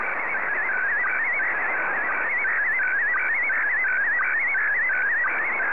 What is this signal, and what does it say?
SSTV (slow-scan television), Scottie DX.
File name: sstv / SDX-VK2OY